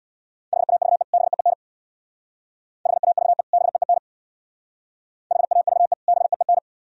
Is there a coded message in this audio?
Yes: Morse code from 0.5 to 6.6 s, heard as a keyed tone, about 63 wpm, 700 Hz. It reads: HAVE BEEN HAVE BEEN HAVE BEEN